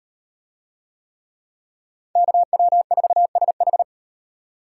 Morse code, heard as a keyed tone, about 35 wpm, 700 Hz. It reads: KW4SH